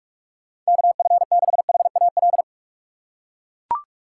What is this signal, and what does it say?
Morse code, heard as a keyed tone, about 42 wpm, 700 Hz. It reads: KF6HAL
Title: USA Call Signs - 4x Character Spacing - 45wpm